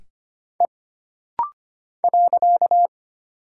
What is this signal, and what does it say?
Morse code, heard as a keyed tone, about 25 wpm, 700 Hz. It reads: E .
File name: Single Character (Letter:Number:Punctuation:Prosign) - Rapid-Fire - 25wpm